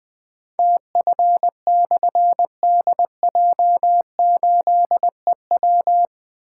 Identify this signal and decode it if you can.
Morse code, heard as a keyed tone, about 20 wpm, 700 Hz. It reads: TF/DJ8EW
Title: Difficult Call Signs 20wpm